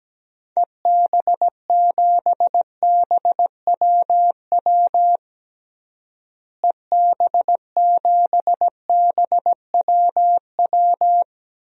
Morse code, heard as a keyed tone, about 17 wpm, 700 Hz. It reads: EB7BWW EB7BWW
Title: DXCC-100 Call Signs - 6 Characters - Repeated 3x - 17wpm